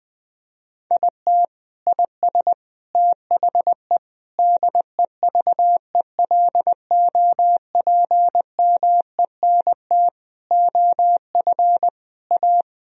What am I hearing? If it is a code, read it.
Morse code, heard as a keyed tone, about 20 wpm, 700 Hz. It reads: IT IS THE DEVELOPMENT OF A
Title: Sentences from Top 1000 Words - Review Part 4 - 20wpm